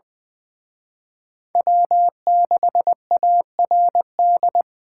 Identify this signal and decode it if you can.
Morse code, heard as a keyed tone, about 24 wpm, 700 Hz. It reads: W6ARD
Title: Speed Racing - USA 1x3 Call Signs 20wpm